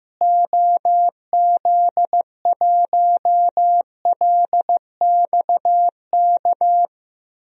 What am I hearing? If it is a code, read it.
Morse code, heard as a keyed tone, about 15 wpm, 700 Hz. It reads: OZ1LXK
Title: DXCC-100 Call Signs - 6 Characters - Repeated 2x - 15wpm